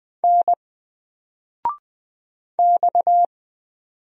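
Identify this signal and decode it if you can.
Morse code, heard as a keyed tone, about 20 wpm, 700 Hz. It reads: N X
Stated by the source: Single letter and number Morse drill at 20 wpm, from N4LCD.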